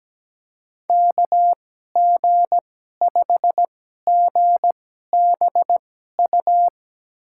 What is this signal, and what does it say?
Morse code, heard as a keyed tone, about 17 wpm, spaced as for 12 wpm, 700 Hz. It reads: KG5GBU
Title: USA Call Signs - 6 Characters - 2x Character Spacing - 17wpm